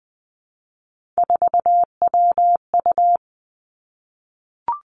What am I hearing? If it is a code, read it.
Morse code, heard as a keyed tone, about 20 wpm, 700 Hz. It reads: 4WU
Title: Three Challenging Letter-Number Combinations 20wpm